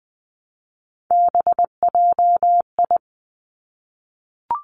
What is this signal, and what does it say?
Morse code, heard as a keyed tone, about 20 wpm, 700 Hz. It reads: BJI
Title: Three Challenging Letter-Number Combinations 20wpm